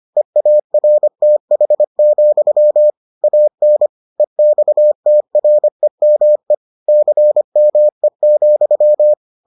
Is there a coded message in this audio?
Yes: Morse code from 0.2 to 9.1 s, heard as a keyed tone, about 25 wpm, 600 Hz. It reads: EARTH, AN EXTREME CME,